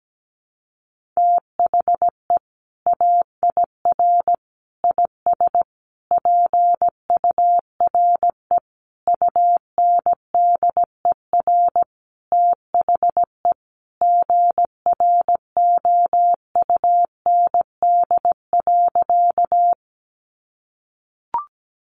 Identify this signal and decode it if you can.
Morse code, heard as a keyed tone, about 17 wpm, 700 Hz. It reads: THE AIR IS PURE UNDER THE GROUND.
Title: Sentences from Top 1500 Words 17wpm